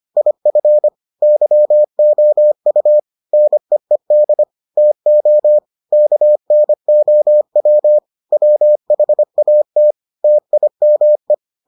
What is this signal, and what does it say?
Morse code, heard as a keyed tone, about 25 wpm, 600 Hz. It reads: IF YOU NEED TO KNOW WHAT TIME